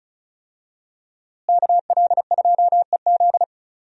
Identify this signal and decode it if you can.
Morse code, heard as a keyed tone, about 36 wpm, 700 Hz. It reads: KL2EZ
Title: USA Call Signs - 5 Characters - Repeated 3x - 35wpm